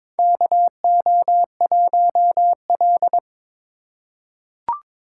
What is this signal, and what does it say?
Morse code, heard as a keyed tone, about 23 wpm, 700 Hz. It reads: KO1L
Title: USA Call Signs - 4 Characters - 3x Character Spacing - 22wpm